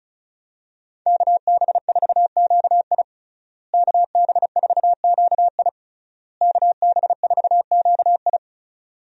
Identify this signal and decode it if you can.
Morse code, heard as a keyed tone, about 36 wpm, 700 Hz. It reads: KB4QI KB4QI KB4QI
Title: USA Call Signs - 5 Characters - Repeated 3x - 35wpm